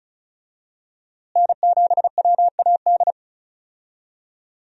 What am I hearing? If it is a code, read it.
Morse code, heard as a keyed tone, about 36 wpm, 700 Hz. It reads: N7WAD
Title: USA Call Signs - 5 Characters - Repeated 2x - 35wpm